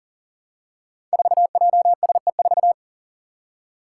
Morse code, heard as a keyed tone, about 38 wpm, 700 Hz. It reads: 4JSE4
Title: Five Challenging Letter-Number Combinations 40wpm